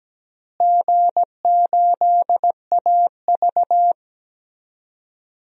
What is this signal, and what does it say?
Morse code, heard as a keyed tone, about 17 wpm, 700 Hz. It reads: G8AV
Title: DXCC-100 Call Signs - 4 Characters - Repeated 2x - 17wpm